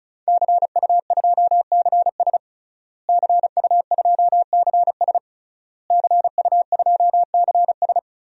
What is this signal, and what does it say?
Morse code, heard as a keyed tone, about 34 wpm, 700 Hz. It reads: CU2CS CU2CS CU2CS
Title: DXCC-100 Call Signs - 5 Characters - Repeated 3x - 35wpm